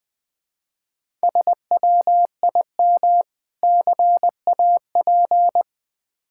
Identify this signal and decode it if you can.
Morse code, heard as a keyed tone, about 20 wpm, 690 Hz. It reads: SWIM CAP
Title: Difficult Words with Hints - 20wpm